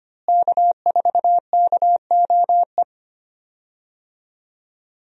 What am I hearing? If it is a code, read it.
Morse code, heard as a keyed tone, about 25 wpm, 700 Hz. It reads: K4KOE
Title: USA Call Signs - 5 Characters - Repeated 3x - 25wpm